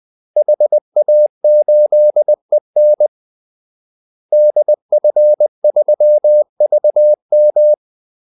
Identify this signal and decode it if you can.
Morse code, heard as a keyed tone, about 20 wpm, 600 Hz. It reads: HA8EN DF3VM
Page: Kaldesignaler 20 wpm